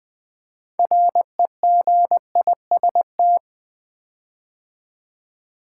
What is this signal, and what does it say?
Morse code, heard as a keyed tone, about 20 wpm, 700 Hz. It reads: REGIST
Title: Build A Word - 6 Letters At A Time - 12-Letter Words - 20wpm